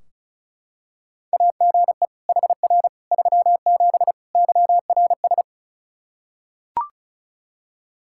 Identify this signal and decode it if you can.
Morse code, heard as a keyed tone, about 36 wpm, 700 Hz. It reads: AGE HR 37 YRS
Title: Practice QSO - Exchange 3 - More Info - 35wpm